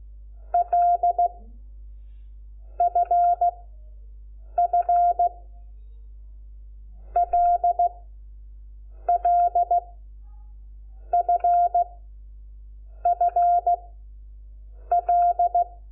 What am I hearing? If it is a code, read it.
Morse code, heard as a keyed tone, about 16 wpm, 700 Hz. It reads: L F F L L F F L
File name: Lecciones / CLASE Nº5